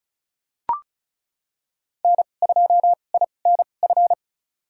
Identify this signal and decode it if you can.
Morse code, heard as a keyed tone, about 36 wpm, spaced as for 25 wpm, 700 Hz. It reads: N2INF
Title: USA Call Signs - 2x Character Spacing - 35wpm